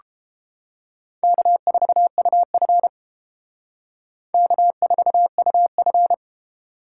Morse code, heard as a keyed tone, about 30 wpm, 700 Hz. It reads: K4UF K4UF